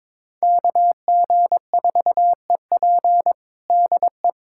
Morse code, heard as a keyed tone, about 22 wpm, 700 Hz. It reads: KG4EP DE
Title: USA Callsign DE Callsign - 5 Characters - 22wpm